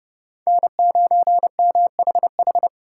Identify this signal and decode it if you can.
Morse code, heard as a keyed tone, about 36 wpm, 700 Hz. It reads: N9MHH
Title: Speed Racing - USA 1x3 Call Signs 30wpm